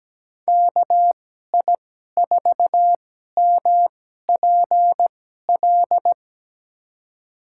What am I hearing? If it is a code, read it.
Morse code, heard as a keyed tone, about 17 wpm, spaced as for 12 wpm, 700 Hz. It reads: KI4MPL
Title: USA Call Signs - 6 Characters - 2x Character Spacing - 17wpm